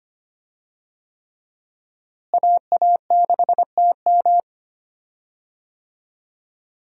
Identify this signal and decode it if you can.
Morse code, heard as a keyed tone, about 25 wpm, 700 Hz. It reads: AA6TM
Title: USA Call Signs - 5 Characters - Repeated 3x - 25wpm